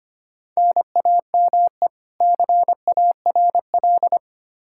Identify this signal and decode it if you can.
Morse code, heard as a keyed tone, about 25 wpm, 700 Hz. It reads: NAME CARL
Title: Practice QSO - Exchange 1 - Beginning - 25wpm